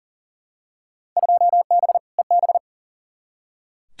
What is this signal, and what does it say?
Morse code, heard as a keyed tone, about 40 wpm, 700 Hz. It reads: 2B EB
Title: Field Day Exchanges - 40wpm